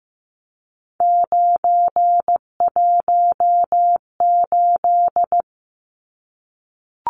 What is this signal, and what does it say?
Morse code, heard as a keyed tone, about 15 wpm, 700 Hz. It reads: 918
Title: Numbers - 3-Digit 15wpm